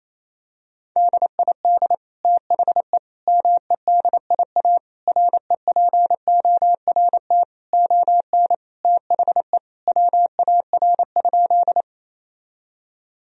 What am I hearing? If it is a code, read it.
Morse code, heard as a keyed tone, about 28 wpm, 700 Hz. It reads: DID THE MEDIA REPORT ON THE WAR?
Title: Sentences from Top 1000 Words - Review Part 4 - 28wpm